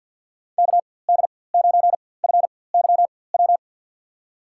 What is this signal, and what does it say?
Morse code, heard as a keyed tone, about 52 wpm, spaced as for 23 wpm, 700 Hz. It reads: KD9FYW
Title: USA Call Signs - 6 Characters - 4x Character Spacing - 50wpm